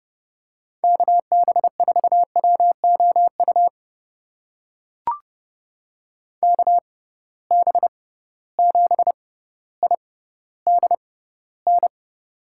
Morse code, heard as a keyed tone, about 30 wpm, spaced as for 15 wpm, 700 Hz. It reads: KB4WOU KB7IDN
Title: USA Call Signs - 6 Characters - 6x Character Spacing - 30wpm